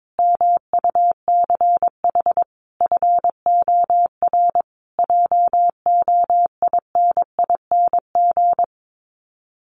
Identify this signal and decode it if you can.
Morse code, heard as a keyed tone, about 22 wpm, 700 Hz. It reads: MUCH FOR JOINING
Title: Sets of 5 Words Encore 22wpm